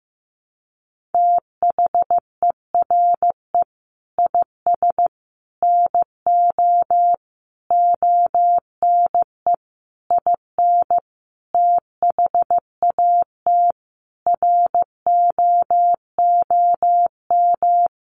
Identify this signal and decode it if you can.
Morse code, heard as a keyed tone, about 15 wpm, 700 Hz. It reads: THERE IS NO ONE IN THAT ROOM
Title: Sentences from Top 500 Words 15wpm